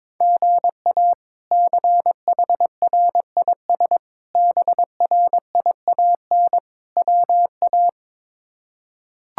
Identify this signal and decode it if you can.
Morse code, heard as a keyed tone, about 22 wpm, 700 Hz. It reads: GA CHRIS BRIAN WA
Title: K1USN SST QSO Practice - 22wpm